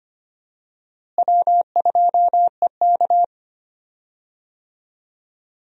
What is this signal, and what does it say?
Morse code, heard as a keyed tone, about 25 wpm, 700 Hz. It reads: W2EK
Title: USA Call Signs - 4 Characters - Repeated 2x - 25wpm